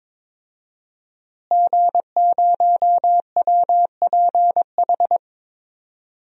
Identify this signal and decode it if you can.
Morse code, heard as a keyed tone, about 22 wpm, 700 Hz. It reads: G0WPH
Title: DXCC-100 Call Signs - 5 Characters - Repeated 3x - 22wpm